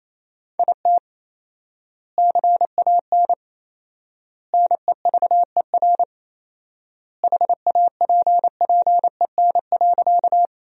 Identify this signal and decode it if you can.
Morse code, heard as a keyed tone, about 28 wpm, 700 Hz. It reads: IT CAN NEVER HAPPEN.
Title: 4x Word-Spacing - Sentences from Top 400 Words - 28wpm